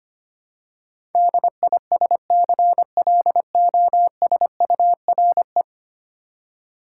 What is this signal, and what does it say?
Morse code, heard as a keyed tone, about 25 wpm, 700 Hz. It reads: DISCLOSURE
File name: 10 Letter Words - First 3 Letters Given - 25wpm